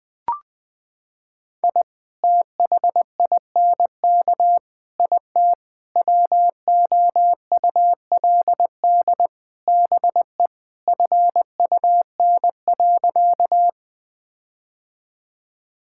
Morse code, heard as a keyed tone, about 20 wpm, 700 Hz. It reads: I THINK IT WOULD BE FUN.
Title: Sentences with 5-Word Phrases - 20wpm